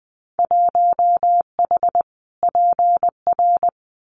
Morse code, heard as a keyed tone, about 20 wpm, 700 Hz. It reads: 1H PR
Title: Winter Field Day Exchanges Encore 20wpm